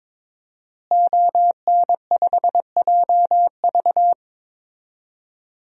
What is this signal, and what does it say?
Morse code, heard as a keyed tone, about 22 wpm, 700 Hz. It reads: ON5JV
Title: DXCC-100 Call Signs - 5 Characters - Repeated 2x - 22wpm